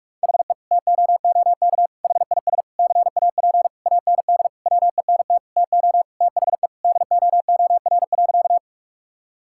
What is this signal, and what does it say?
Morse code, heard as a keyed tone, about 46 wpm, 700 Hz. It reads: HE TOOK HIS CAP AND WENT TO THE DOOR.
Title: Sentences from Top 1600 Words 45wpm